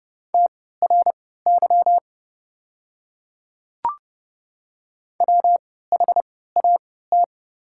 Morse code, heard as a keyed tone, about 29 wpm, spaced as for 17 wpm, 700 Hz. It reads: TRY WHAT
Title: 3x Character Spacing - Top 200 Words - 30wpm